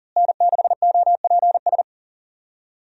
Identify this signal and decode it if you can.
Morse code, heard as a keyed tone, about 40 wpm, 700 Hz. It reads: N6OPS
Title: USA Call Signs - 5 Characters - Repeated 2x - 40wpm